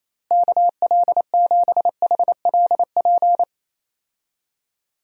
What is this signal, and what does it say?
Morse code, heard as a keyed tone, about 27 wpm, 700 Hz. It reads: KL7HLP
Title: DXCC-100 Call Signs - 6 Characters - Repeated 3x - 28wpm